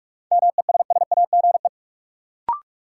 Morse code, heard as a keyed tone, about 46 wpm, 700 Hz. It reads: MESSAGE
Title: Sentences from Top 1100 Words - New Word Drill Alt - 45wpm